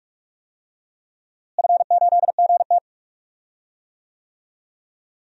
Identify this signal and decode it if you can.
Morse code, heard as a keyed tone, about 44 wpm, 700 Hz. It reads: F8GT
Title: DXCC-100 Call Signs - 4 Characters - Repeated 3x - 45wpm